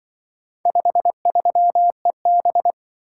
Morse code, heard as a keyed tone, about 24 wpm, 700 Hz. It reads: 53EB
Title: Speed-Racing - Four Letter-Number Combinations - 20wpm